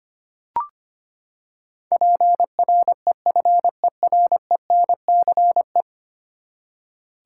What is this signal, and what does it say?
Morse code, heard as a keyed tone, about 25 wpm, 700 Hz. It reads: PREFERENCE
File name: Ten Letter Words - 25wpm